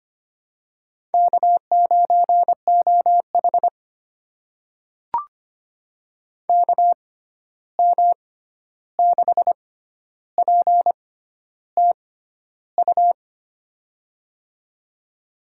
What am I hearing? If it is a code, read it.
Morse code, heard as a keyed tone, about 25 wpm, spaced as for 11 wpm, 700 Hz. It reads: K9OH KM6PTU
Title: USA Call Signs - 6x Character Spacing - 25wpm